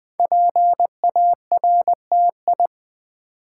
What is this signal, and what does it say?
Morse code, heard as a keyed tone, about 20 wpm, 700 Hz. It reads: PARTI
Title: Build A Word - 5 Letters At A Time - 12-Letter Words - 20wpm